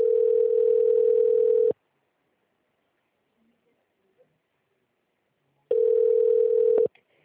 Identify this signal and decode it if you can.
Ringback tone (North America), 440 + 480 Hz.